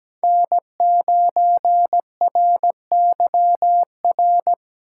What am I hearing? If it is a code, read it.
Morse code, heard as a keyed tone, about 17 wpm, 700 Hz. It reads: N9RYR
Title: USA Call Signs - 4x Character Spacing - 17wpm